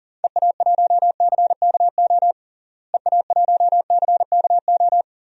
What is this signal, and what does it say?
Morse code, heard as a keyed tone, about 40 wpm, 700 Hz. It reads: EA1CKO EA1CKO
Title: DXCC-100 Call Signs - 6 Characters - Repeated 2x - 40wpm